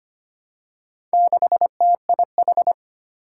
Morse code, heard as a keyed tone, about 25 wpm, 700 Hz. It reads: 6TIH
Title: Four Challenging Letter-Number Combinations 25wpm